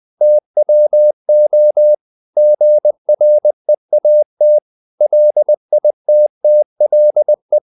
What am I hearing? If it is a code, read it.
Morse code, heard as a keyed tone, about 20 wpm, 600 Hz. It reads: TWO GREAT LITTLE
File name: CW / Practice / 500MostCommonEnglishWords20